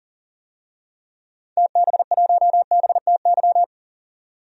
Morse code, heard as a keyed tone, about 39 wpm, 700 Hz. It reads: TB1BTY
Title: DXCC-100 Call Signs - 6 Characters - 40wpm